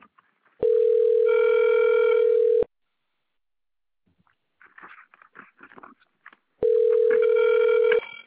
Ringback tone (North America), 440 + 480 Hz.